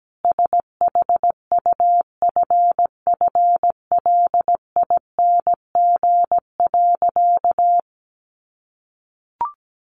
Morse code, heard as a keyed tone, about 17 wpm, 700 Hz. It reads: SHUFFLING.
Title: A Sound Of Thunder 17wpm